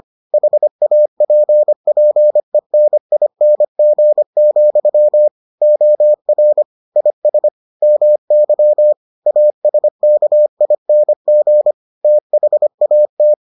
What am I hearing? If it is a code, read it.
Morse code, heard as a keyed tone, about 25 wpm, 600 Hz. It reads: HAPPENING, OR IS MY ASKING THAT